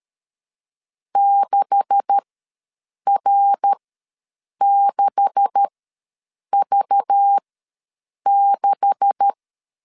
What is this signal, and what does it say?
Morse code, heard as a keyed tone, about 13 wpm, spaced as for 7 wpm, 800 Hz. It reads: 6R6V6
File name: LEZIONI / 9 286 ws 12wpm 3351